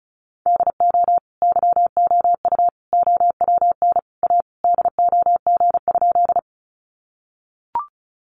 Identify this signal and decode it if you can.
Morse code, heard as a keyed tone, about 34 wpm, 700 Hz. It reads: DO YOU OWN A DOG?
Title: Sentences from Top 400 Words 35wpm